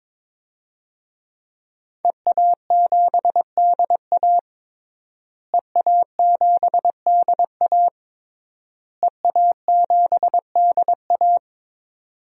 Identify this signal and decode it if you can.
Morse code, heard as a keyed tone, about 22 wpm, 700 Hz. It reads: EA7DA EA7DA EA7DA
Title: DXCC-100 Call Signs - 5 Characters - Repeated 3x - 22wpm